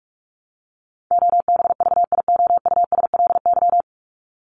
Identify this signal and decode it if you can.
Morse code, heard as a keyed tone, about 44 wpm, 700 Hz. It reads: OBVIOUSLY
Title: LY Suffix - Top 100 Words with Base Words 45wpm